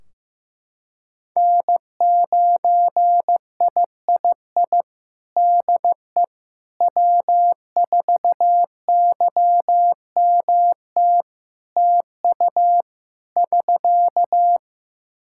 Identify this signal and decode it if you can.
Morse code, heard as a keyed tone, about 16 wpm, 700 Hz. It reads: N9III DE W4YMT TU <SK>
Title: Practice QSO - Exchange 4 - Ending - 15wpm